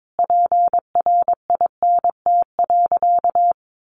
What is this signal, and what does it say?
Morse code, heard as a keyed tone, about 22 wpm, 700 Hz. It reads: PRINT.
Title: Sentences from Top 1400 Words 22wpm